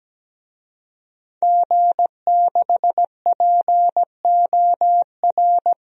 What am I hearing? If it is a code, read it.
Morse code, heard as a keyed tone, about 17 wpm, 700 Hz. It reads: G6POR
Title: DXCC-100 Call Signs - 5 Characters - 17wpm